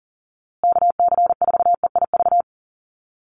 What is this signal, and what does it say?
Morse code, heard as a keyed tone, about 40 wpm, 700 Hz. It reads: KC4EIV
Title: USA Call Signs Repeated 3x 40wpm